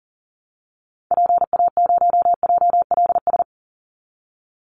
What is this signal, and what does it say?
Morse code, heard as a keyed tone, about 40 wpm, 700 Hz. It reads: PA0JLS
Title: Call Signs Repeated 3x - 40wpm